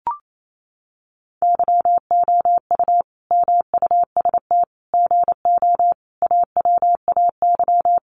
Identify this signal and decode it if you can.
Morse code, heard as a keyed tone, about 28 wpm, 700 Hz. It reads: YOU MUST GO AWAY
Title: Sentences from Top 1100 Words 28wpm